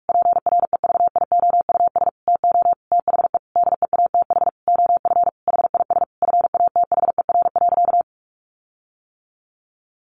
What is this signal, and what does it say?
Morse code, heard as a keyed tone, about 44 wpm, 700 Hz. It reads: PREVIOUS TO THE DEATH OF HIS FATHER.
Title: Sentences from Top 1500 Words 45wpm